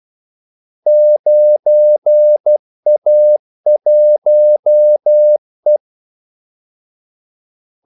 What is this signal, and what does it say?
Morse code, heard as a keyed tone, about 12 wpm, 600 Hz. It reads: 9A1E
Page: 2 ekstra ordmellemrum mellem kaldesignalerne